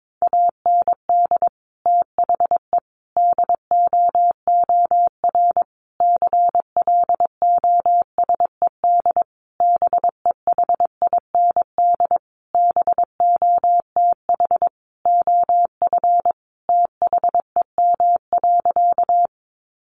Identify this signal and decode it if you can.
Morse code, heard as a keyed tone, about 22 wpm, 700 Hz. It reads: AND THE DOOR CLOSED BEHIND BOTH OF THEM.
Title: Sentences from Top 1400 Words 22wpm